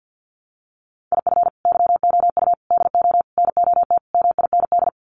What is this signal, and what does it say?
Morse code, heard as a keyed tone, about 48 wpm, 700 Hz. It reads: IF YOU DO NOT MIND
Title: Sets of 5 Words 50wpm